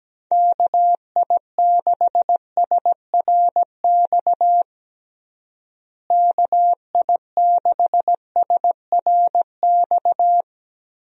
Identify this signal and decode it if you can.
Morse code, heard as a keyed tone, about 17 wpm, 700 Hz. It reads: KI6SRX KI6SRX
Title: USA Call Signs - 6 Characters - Repeated 2x - 17wpm